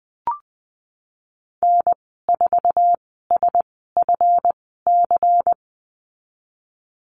Morse code, heard as a keyed tone, about 20 wpm, spaced as for 15 wpm, 700 Hz. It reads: N4SFC
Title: 147 Lesson 24g - Letter y - Call Signs 20wpm